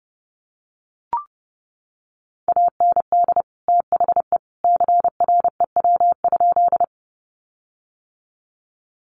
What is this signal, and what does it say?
Morse code, heard as a keyed tone, about 30 wpm, 700 Hz. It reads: AND THE CREW?
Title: Sentences from Top 1500 Words 30wpm